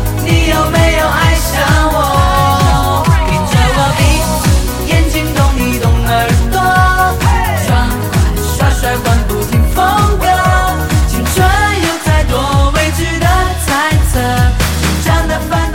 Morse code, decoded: none